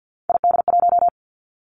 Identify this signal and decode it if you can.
Morse code, heard as a keyed tone, about 50 wpm, 700 Hz. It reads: ID1
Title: Three Challenging Letter-Number Combinations 50wpm